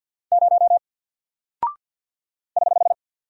Morse code, heard as a keyed tone, about 48 wpm, 700 Hz. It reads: 0 <HH>
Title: Single Character (Letter:Number:Punctuation:Prosign) Alt 50wpm